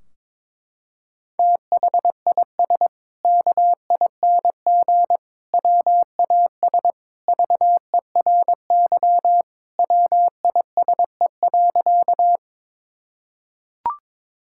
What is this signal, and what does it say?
Morse code, heard as a keyed tone, about 22 wpm, 700 Hz. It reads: THIS KING WAS VERY WISE.
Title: Sentences from Top 1600 Words 22wpm